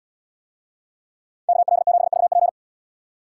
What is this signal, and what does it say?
Morse code, heard as a keyed tone, about 74 wpm, 700 Hz. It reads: KL7VP